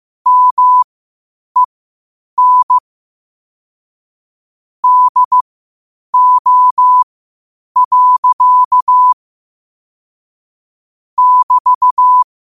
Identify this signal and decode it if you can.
Morse code, heard as a keyed tone, about 15 wpm, spaced as for 8 wpm, 1000 Hz. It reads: MEN DO. <BT>
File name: qotdfiles / 8wpm